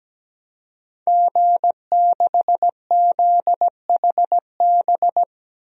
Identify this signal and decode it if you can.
Morse code, heard as a keyed tone, about 17 wpm, 700 Hz. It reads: G6ZHB
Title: DXCC-100 Call Signs - 5 Characters - 17wpm